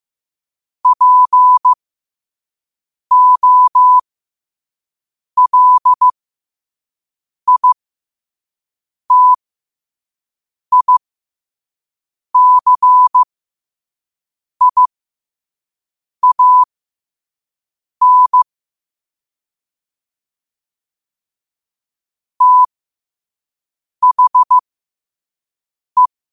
Morse code, heard as a keyed tone, about 15 wpm, spaced as for 5 wpm, 1000 Hz. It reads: POLITICIAN THE
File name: qotdfiles / 5wpm